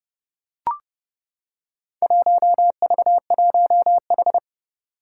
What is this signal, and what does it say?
Morse code, heard as a keyed tone, about 30 wpm, 700 Hz. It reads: 1V1H
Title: Four Challenging Letter-Number Combinations 30wpm